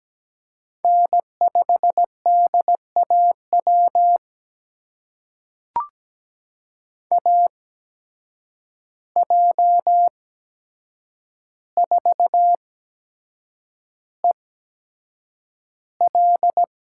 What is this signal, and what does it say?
Morse code, heard as a keyed tone, about 17 wpm, spaced as for 7 wpm, 700 Hz. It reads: N5DAW AJ4EL
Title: USA Call Signs - 5 Characters - 8x Character Spacing - 17wpm